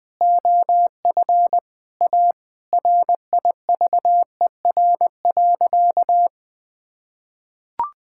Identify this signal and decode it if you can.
Morse code, heard as a keyed tone, about 20 wpm, 700 Hz. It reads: OF A RIVER.
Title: Sentences from Top 1000 Words - Review Part 6 - 20wpm